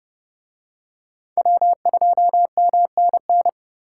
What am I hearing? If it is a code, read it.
Morse code, heard as a keyed tone, about 29 wpm, 700 Hz. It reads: W2MNN